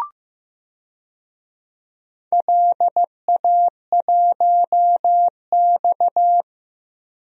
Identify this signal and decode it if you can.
Morse code, heard as a keyed tone, about 15 wpm, 700 Hz. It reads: LA1X
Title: DXCC-100 Call Signs - 4 Characters - 15wpm